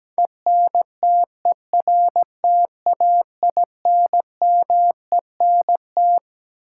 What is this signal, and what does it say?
Morse code, heard as a keyed tone, about 17 wpm, 700 Hz. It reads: ENTERTAINMENT
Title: Sentences from Top 1000 Words - Review Part 6 - New Word Drill - 17wpm